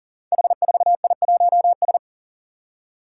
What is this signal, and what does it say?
Morse code, heard as a keyed tone, about 40 wpm, 700 Hz. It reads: H4I1S
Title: Five Challenging Letter-Number Combinations 40wpm